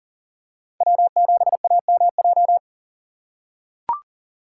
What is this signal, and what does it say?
Morse code, heard as a keyed tone, about 39 wpm, 700 Hz. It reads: W7AMJ
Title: USA Call Signs - 5 Characters - 2x Character Spacing - 40wpm